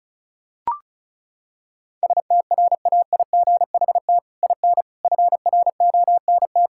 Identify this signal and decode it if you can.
Morse code, heard as a keyed tone, about 36 wpm, 700 Hz. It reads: STRAIGHT IN FRONT
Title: Sentences from Top 1000 Words - Review Part 6 - 35wpm